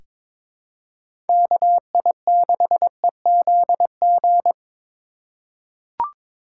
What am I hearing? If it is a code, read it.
Morse code, heard as a keyed tone, about 23 wpm, 700 Hz. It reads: KI6EZG
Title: USA Call Signs - 2x Character Spacing - 22wpm